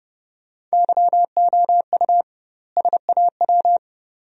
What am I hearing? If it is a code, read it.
Morse code, heard as a keyed tone, about 30 wpm, 700 Hz. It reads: YOU SAW
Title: 2x Word-Spacing - Sentences from Top 400 Words - 30wpm